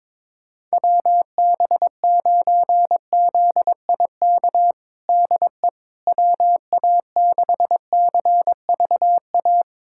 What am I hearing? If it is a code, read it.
Morse code, heard as a keyed tone, about 21 wpm, 700 Hz. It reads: WB9ZIK DE WA6CVA
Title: USA Callsign DE Callsign - 6 Characters - 22wpm